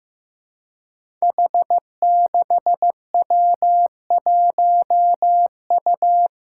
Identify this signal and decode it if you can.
Morse code, heard as a keyed tone, about 15 wpm, 700 Hz. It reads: H6W1U
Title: Five Challenging Letter-Number Combinations 15wpm